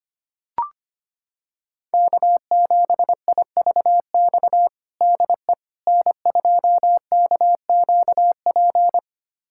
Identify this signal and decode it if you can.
Morse code, heard as a keyed tone, about 25 wpm, 700 Hz. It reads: K7IVX DE N2KQP
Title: USA Callsign DE Callsign - 5 Characters - 25wpm